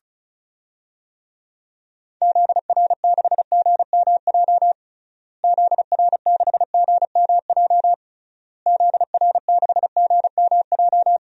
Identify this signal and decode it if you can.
Morse code, heard as a keyed tone, about 36 wpm, 700 Hz. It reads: ZR6GMJ ZR6GMJ ZR6GMJ
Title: DXCC-100 Call Signs - 6 Characters - Repeated 3x - 35wpm